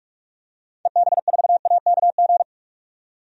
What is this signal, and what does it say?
Morse code, heard as a keyed tone, about 47 wpm, 700 Hz. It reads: EB4AKG
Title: DXCC-100 Call Signs - 6 Characters - 45wpm